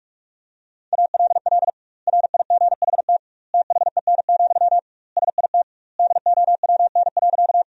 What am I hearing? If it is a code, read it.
Morse code, heard as a keyed tone, about 41 wpm, 700 Hz. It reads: ALL RIGHT THEN, SIT DOWN.
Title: 2x Word-Spacing - Sentences from Top 500 Words - 45wpm